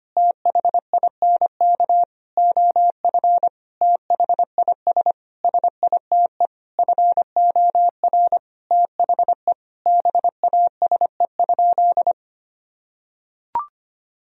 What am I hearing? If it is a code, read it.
Morse code, heard as a keyed tone, about 25 wpm, 700 Hz. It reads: THINK OF THIS SITE FOR THE BASE?
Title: Sentences from Top 1000 Words - Review Part 1 - 25wpm